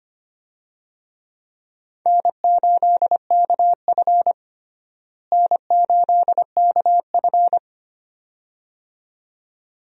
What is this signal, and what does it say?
Morse code, heard as a keyed tone, about 25 wpm, 700 Hz. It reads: N8KF N8KF
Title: USA Call Signs - 4 Characters - Repeated 2x - 25wpm